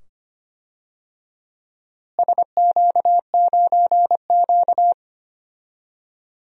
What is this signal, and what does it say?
Morse code, heard as a keyed tone, about 25 wpm, 700 Hz. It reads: SQ9Q
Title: DXCC-100 Call Signs - 4 Characters - Repeated 2x - 25wpm